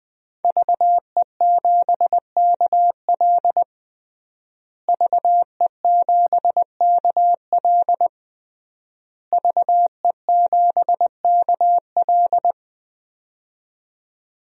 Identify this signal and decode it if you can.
Morse code, heard as a keyed tone, about 20 wpm, 700 Hz. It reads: VE7KL VE7KL VE7KL
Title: DXCC-100 Call Signs - 5 Characters - Repeated 3x - 20wpm